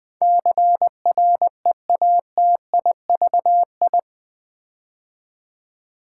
Morse code, heard as a keyed tone, about 20 wpm, 700 Hz. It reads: CREATIVI
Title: Build A Word - 4 Letters At A Time - 10-Letter Words - 20wpm